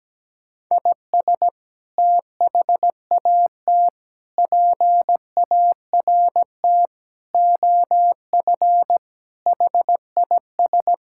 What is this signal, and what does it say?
Morse code, heard as a keyed tone, about 17 wpm, 700 Hz. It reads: IS THAT PART OF HIS
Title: Sentences from Top 1000 Words - Review Part 5 - 17wpm